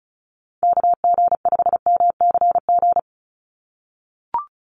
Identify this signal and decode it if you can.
Morse code, heard as a keyed tone, about 36 wpm, 700 Hz. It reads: KG5MCG
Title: USA Call Signs 35wpm